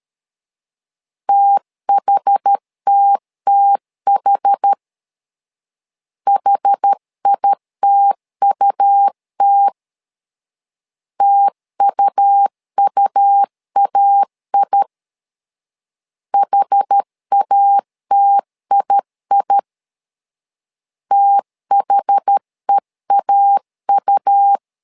Morse code, heard as a keyed tone, about 13 wpm, 800 Hz. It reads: THTTH HITUT TUUAI HATII THEAU
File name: LEZIONI / 2 UHDN / 2 UH 12wpm 1439